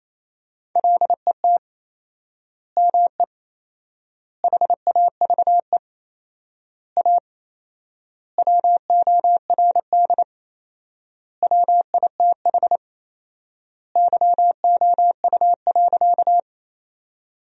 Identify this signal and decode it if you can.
Morse code, heard as a keyed tone, about 28 wpm, 700 Hz. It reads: LET ME HAVE A WORD WITH YOU.
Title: 4x Word-Spacing - Sentences from Top 300 Words - 28wpm